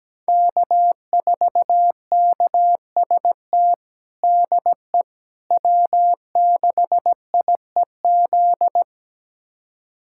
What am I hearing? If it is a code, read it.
Morse code, heard as a keyed tone, about 17 wpm, 700 Hz. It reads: K4KST DE W6IEZ
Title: USA Callsign DE Callsign - 5 Characters - 17wpm